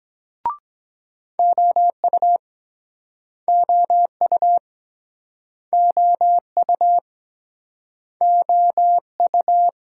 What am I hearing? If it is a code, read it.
Morse code, heard as a keyed tone, about 21 wpm, 700 Hz. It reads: OU OU OU OU
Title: Speed-Racing Multi-letter Phonemes - 17wpm